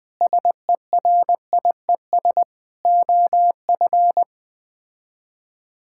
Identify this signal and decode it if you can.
Morse code, heard as a keyed tone, about 21 wpm, 700 Hz. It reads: SERIES OF
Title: Sets of 8 Words - 20wpm